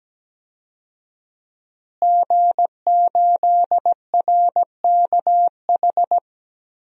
Morse code, heard as a keyed tone, about 17 wpm, 700 Hz. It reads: G8RKH
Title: DXCC-100 Call Signs - 5 Characters - Repeated 2x - 17wpm